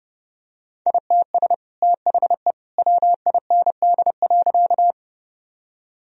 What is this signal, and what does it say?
Morse code, heard as a keyed tone, about 32 wpm, 700 Hz. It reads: ITS THE WIND.
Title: Sentences from Top 1000 Words - Review Part 2 - 30wpm